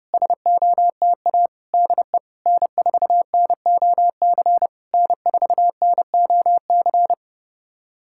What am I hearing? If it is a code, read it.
Morse code, heard as a keyed tone, about 30 wpm, 690 Hz. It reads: SOTA DE N4NOC N4NOC
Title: SOTA QSO Practice 30wpm